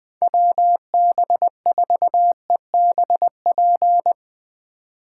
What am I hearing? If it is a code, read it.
Morse code, heard as a keyed tone, about 20 wpm, 700 Hz. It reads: WB4EBP
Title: USA Call Signs - 6 Characters - 2x Character Spacing - 20wpm